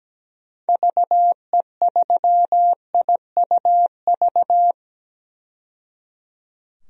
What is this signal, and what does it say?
Morse code, heard as a keyed tone, about 17 wpm, 700 Hz. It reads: VE3IUV
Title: DXCC-100 Call Signs - 6 Characters - 17wpm